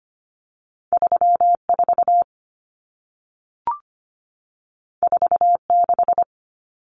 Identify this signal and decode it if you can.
Morse code, heard as a keyed tone, about 25 wpm, 700 Hz. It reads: 34 46
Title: Common Cut Numbers - 2-Digit 25wpm